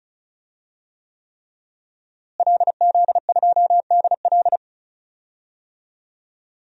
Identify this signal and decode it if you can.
Morse code, heard as a keyed tone, about 34 wpm, 700 Hz. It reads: LZ2DL
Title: DXCC-100 Call Signs - 5 Characters - Repeated 3x - 35wpm